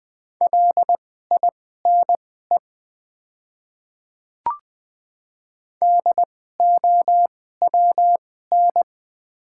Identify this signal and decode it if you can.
Morse code, heard as a keyed tone, about 20 wpm, spaced as for 14 wpm, 700 Hz. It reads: LINE DOWN
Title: 2x Character Spacing - Top 200 Words - 20wpm